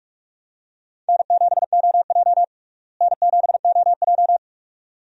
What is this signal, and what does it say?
Morse code, heard as a keyed tone, about 46 wpm, 700 Hz. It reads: N7OJ N7OJ
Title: USA Call Signs - 4 Characters - Repeated 2x - 45wpm